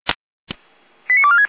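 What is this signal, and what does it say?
Five-tone selective call (ZVEI), heard as quick runs of five tones, tones of about 80 ms. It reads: 89246
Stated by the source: ZVEI five-tone selective call received by SDR.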